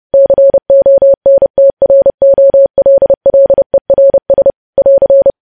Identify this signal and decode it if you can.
Morse code, heard as a keyed tone, about 30 wpm, 550 Hz. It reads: CONTROLLERS <AR>
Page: news headlines in morse code at 30 wpm